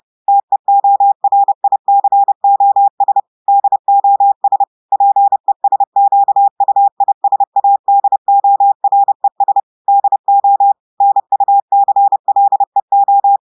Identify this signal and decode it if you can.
Morse code, heard as a keyed tone, about 30 wpm, 800 Hz. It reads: TEORICOS DOS PESQUISADORES DO NUCLEO
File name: audios / portugais / portugais texte en clair 1 vitesse 30 mots minutes